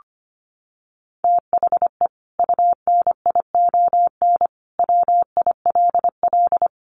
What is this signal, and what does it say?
Morse code, heard as a keyed tone, about 25 wpm, 700 Hz. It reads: THE UNION WILL
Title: Sentences from Top 1500 Words 25wpm